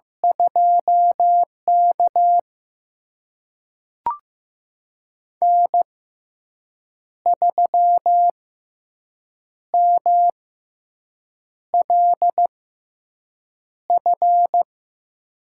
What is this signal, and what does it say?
Morse code, heard as a keyed tone, about 15 wpm, spaced as for 6 wpm, 700 Hz. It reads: 2K N3MLF
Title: USA Call Signs - 6x Character Spacing - 15wpm